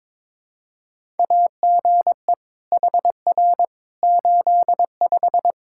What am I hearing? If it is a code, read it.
Morse code, heard as a keyed tone, about 22 wpm, 700 Hz. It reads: AGE HR 85
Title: Practice QSO - Exchange 3 - More Info - 22wpm